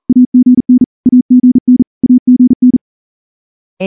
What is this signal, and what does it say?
Morse code, heard as a keyed tone, about 38 wpm, 250 Hz. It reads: AGN AGN AGN
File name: morse-training-40WPM